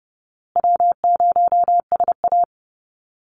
Morse code, heard as a keyed tone, about 30 wpm, 700 Hz. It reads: W0SA
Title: Call Signs Repeated 3x - 30wpm